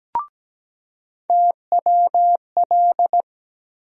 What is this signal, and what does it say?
Morse code, heard as a keyed tone, about 17 wpm, 700 Hz. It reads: TWL
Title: USA Call Sign Postfixes - 3 Letters 17wpm